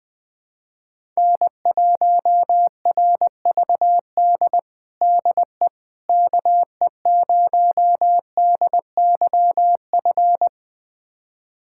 Morse code, heard as a keyed tone, about 20 wpm, 700 Hz. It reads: N1RVD DE KE0DYF
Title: USA Callsign DE Callsign 20wpm